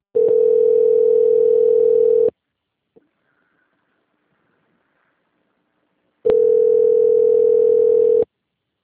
Ringback tone (North America), 440 + 480 Hz.
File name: out- - - - - .52982